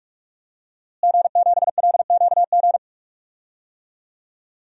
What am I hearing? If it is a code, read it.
Morse code, heard as a keyed tone, about 45 wpm, 700 Hz. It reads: G7LQG
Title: DXCC-100 Call Signs - 5 Characters - Repeated 2x - 45wpm